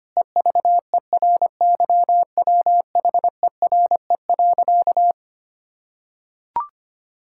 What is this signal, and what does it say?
Morse code, heard as a keyed tone, about 25 wpm, 700 Hz. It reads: EVERYWHERE.
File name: practice / 25wpm / Difficult Sentences with Hints - 25wpm